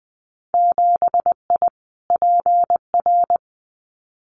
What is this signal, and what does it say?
Morse code, heard as a keyed tone, about 19 wpm, 700 Hz. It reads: 7I PR
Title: Winter Field Day Exchanges Encore 20wpm